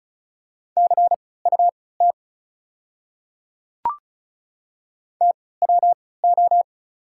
Morse code, heard as a keyed tone, about 36 wpm, spaced as for 20 wpm, 690 Hz. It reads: CUT TWO
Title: 3x Character Spacing - Top 300 Words - 35wpm